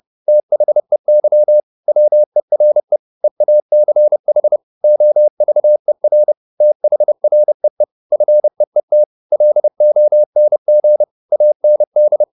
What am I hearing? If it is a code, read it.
Morse code, heard as a keyed tone, about 30 wpm, 600 Hz. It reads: THEY WERE EACH OVER THREE FEET LONG AND